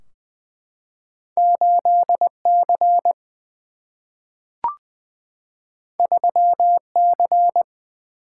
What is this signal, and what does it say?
Morse code, heard as a keyed tone, about 20 wpm, 700 Hz. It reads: 8C 3C
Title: Single-Letter Single-Number 20wpm